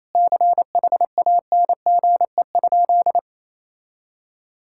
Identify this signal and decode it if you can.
Morse code, heard as a keyed tone, about 28 wpm, 700 Hz. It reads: CHANGE?
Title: Sentences from Top 1600 Words 28wpm